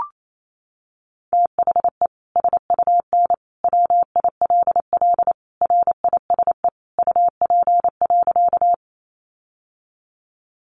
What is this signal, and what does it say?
Morse code, heard as a keyed tone, about 28 wpm, 700 Hz. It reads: THE SUN WILL RISE UP.
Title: Sentences from Top 900 Words 28wpm